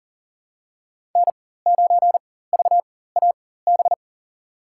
Morse code, heard as a keyed tone, about 40 wpm, spaced as for 19 wpm, 700 Hz. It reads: N9VAB
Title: USA Call Signs - 4x Character Spacing - 40wpm